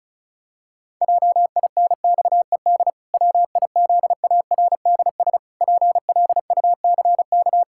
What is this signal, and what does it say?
Morse code, heard as a keyed tone, about 34 wpm, 700 Hz. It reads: JINXED WIZARDS PLUCK
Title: Pangram Challenge v2 - 35wpm